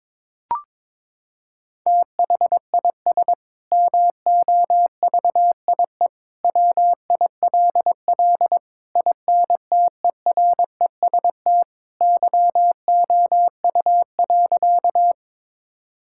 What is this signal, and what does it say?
Morse code, heard as a keyed tone, about 22 wpm, 700 Hz. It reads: THIS MOVIE WILL INTEREST YOU.
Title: Sentences from Top 1000 Words - Review Part 5 - 22wpm